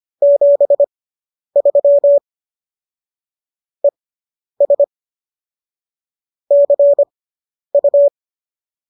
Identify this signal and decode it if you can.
Morse code, heard as a keyed tone, about 25 wpm, spaced as for 10 wpm, 570 Hz. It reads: 73 ES CU